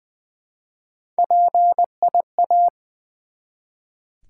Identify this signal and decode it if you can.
Morse code, heard as a keyed tone, about 20 wpm, 700 Hz. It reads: PIA
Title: Build A Word - 1 Letter At A Time - 3 to 5-Letter Words - 20wpm